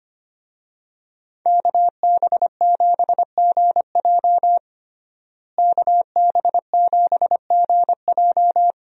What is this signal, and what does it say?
Morse code, heard as a keyed tone, about 25 wpm, 700 Hz. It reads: KB7GJ KB7GJ
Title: USA Call Signs - 5 Characters - Repeated 3x - 25wpm